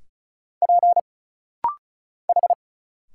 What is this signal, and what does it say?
Morse code, heard as a keyed tone, about 35 wpm, 700 Hz. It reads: P H
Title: Single Letters - Rapid-Fire 35wpm